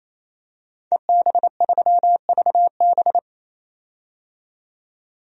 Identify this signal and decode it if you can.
Morse code, heard as a keyed tone, about 28 wpm, 700 Hz. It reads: EB3VB
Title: DXCC-100 Call Signs - 5 Characters - 28wpm